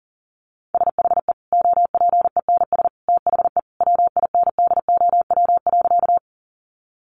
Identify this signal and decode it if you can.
Morse code, heard as a keyed tone, about 40 wpm, 700 Hz. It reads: SHE OPENS THE WINDOW.
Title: Additional Easy Phrases 40wpm